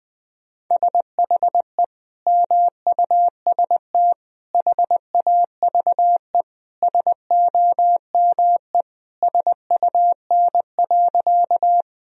Morse code, heard as a keyed tone, about 19 wpm, 700 Hz. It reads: SHE MUST HAVE SOME SUN.
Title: Sentences from Top 1000 Words - Review Part 1 - 20wpm